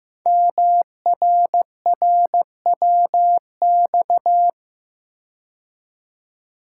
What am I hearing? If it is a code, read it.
Morse code, heard as a keyed tone, about 15 wpm, 700 Hz. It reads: MRRWX
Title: Code Groups - 5 Letters - 15wpm